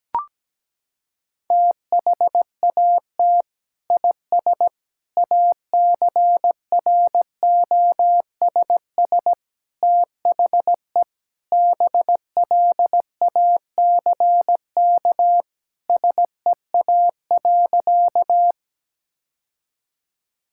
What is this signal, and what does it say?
Morse code, heard as a keyed tone, about 17 wpm, 700 Hz. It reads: THAT IS ACROSS THE BLACK SEA.
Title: Sentences from Top 1000 Words - Review Part 5 - 17wpm